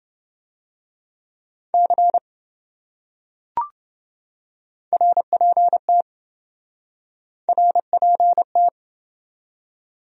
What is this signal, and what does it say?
Morse code, heard as a keyed tone, about 26 wpm, 700 Hz. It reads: C RPT RPT